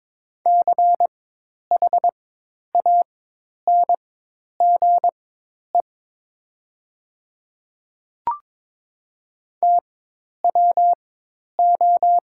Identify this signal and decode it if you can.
Morse code, heard as a keyed tone, about 22 wpm, spaced as for 10 wpm, 700 Hz. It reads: CHANGE TWO
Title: 4x Character Spacing - Top 300 Words - 22wpm